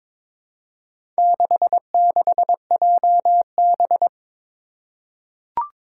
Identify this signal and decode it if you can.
Morse code, heard as a keyed tone, about 22 wpm, 700 Hz. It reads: 66JB
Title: Four Challenging Letter-Number Combinations 22wpm